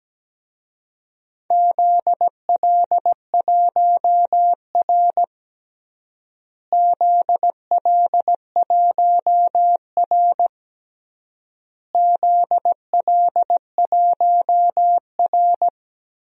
Morse code, heard as a keyed tone, about 17 wpm, 700 Hz. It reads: ZL1R ZL1R ZL1R
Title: DXCC-100 Call Signs - 4 Characters - Repeated 3x - 17wpm